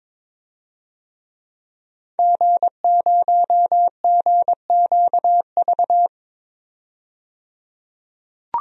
Morse code, heard as a keyed tone, about 22 wpm, 700 Hz. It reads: G0GQV
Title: DXCC-100 Call Signs - 5 Characters - 22wpm